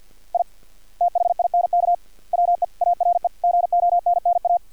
Morse code, heard as a keyed tone, about 50 wpm, 700 Hz. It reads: I THINK WE ARE GONNA
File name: archived / CRSnet